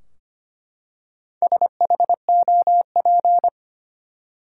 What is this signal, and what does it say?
Morse code, heard as a keyed tone, about 25 wpm, 700 Hz. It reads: SHOP
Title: Four Letter Words - 25wpm